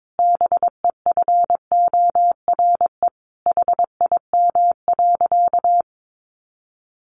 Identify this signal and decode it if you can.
Morse code, heard as a keyed tone, about 22 wpm, 700 Hz. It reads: BEFORE HIM.
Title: Sentences from Top 1100 Words 22wpm